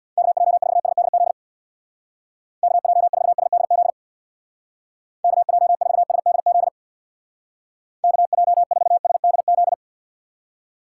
Morse code, heard as a keyed tone, about 60 wpm, 700 Hz. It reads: KJ4SDZ KJ4SDZ KJ4SDZ KJ4SDZ
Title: Speed Racing - USA 2x3 Call Signs 50wpm